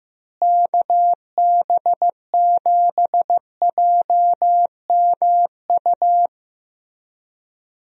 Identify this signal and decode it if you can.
Morse code, heard as a keyed tone, about 16 wpm, 700 Hz. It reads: KB7JMU
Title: USA Call Signs - 6 Characters - Repeated 3x - 15wpm